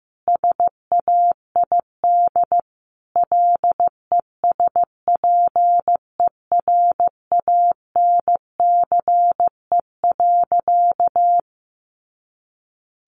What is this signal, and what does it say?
Morse code, heard as a keyed tone, about 15 wpm, 700 Hz. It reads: SAID LESPERANCE.
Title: A Sound Of Thunder 15wpm